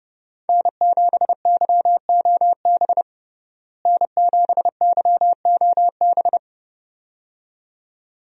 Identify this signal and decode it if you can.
Morse code, heard as a keyed tone, about 30 wpm, 700 Hz. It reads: N7YOB N7YOB
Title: USA Call Signs - 5 Characters - Repeated 2x - 30wpm